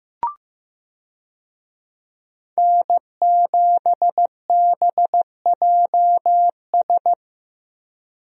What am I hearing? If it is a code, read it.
Morse code, heard as a keyed tone, about 15 wpm, 700 Hz. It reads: N7BJS
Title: USA Call Signs - 5 Characters - Repeated 2x - 15wpm